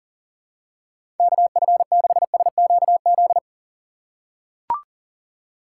Morse code, heard as a keyed tone, about 40 wpm, 700 Hz. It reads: KF6SQZ
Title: USA Call Signs - 6 Characters - 2x Character Spacing - 40wpm